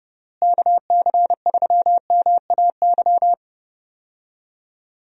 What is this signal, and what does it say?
Morse code, heard as a keyed tone, about 32 wpm, 700 Hz. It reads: KC3MAY
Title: USA Call Signs - 6 Characters - Repeated 3x - 30wpm